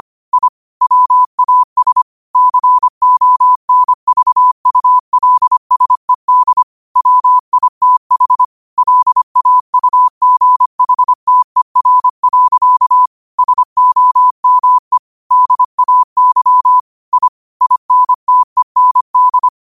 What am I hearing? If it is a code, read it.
Morse code, heard as a keyed tone, about 25 wpm, 1000 Hz. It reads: I WAS CONVULSED WITH LAUGHTER. SOME DAY I INTEND
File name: qotdfiles / 25wpm